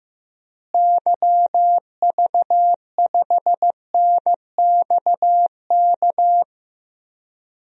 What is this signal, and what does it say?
Morse code, heard as a keyed tone, about 15 wpm, 700 Hz. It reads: YV5NXK
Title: DXCC-100 Call Signs - 6 Characters - Repeated 2x - 15wpm